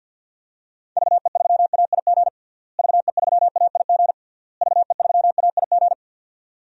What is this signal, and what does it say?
Morse code, heard as a keyed tone, about 50 wpm, 700 Hz. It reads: VE3AIG VE3AIG VE3AIG
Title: DXCC-100 Call Signs - 6 Characters - Repeated 3x - 50wpm